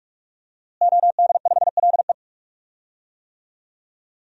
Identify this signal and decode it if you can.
Morse code, heard as a keyed tone, about 43 wpm, 700 Hz. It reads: OD5LE
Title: DXCC-100 Call Signs - 5 Characters - Repeated 2x - 45wpm